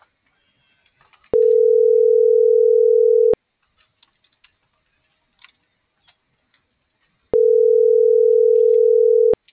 Ringback tone (North America), 440 + 480 Hz.